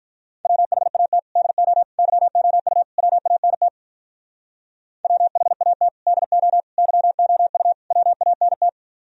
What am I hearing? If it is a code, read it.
Morse code, heard as a keyed tone, about 44 wpm, 700 Hz. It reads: WHAT DO YOU WANT WHAT DO YOU WANT